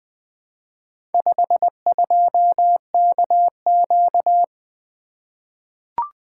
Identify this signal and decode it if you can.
Morse code, heard as a keyed tone, about 20 wpm, 700 Hz. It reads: 52KQ
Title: Four Letter-Number Combinations - 20wpm